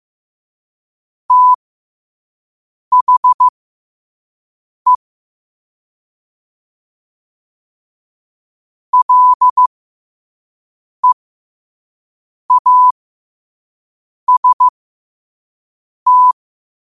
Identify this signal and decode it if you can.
Morse code, heard as a keyed tone, about 15 wpm, spaced as for 5 wpm, 1000 Hz. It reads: THE LEAST
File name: qotdfiles / 5wpm